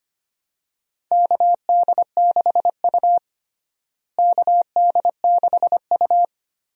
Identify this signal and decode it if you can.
Morse code, heard as a keyed tone, about 25 wpm, 700 Hz. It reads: KD6U KD6U
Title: USA Call Signs - 4 Characters - Repeated 3x - 25wpm